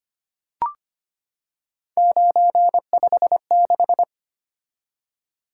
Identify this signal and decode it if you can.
Morse code, heard as a keyed tone, about 25 wpm, 700 Hz. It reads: 956
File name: Three Letter-Number Combinations - 25wpm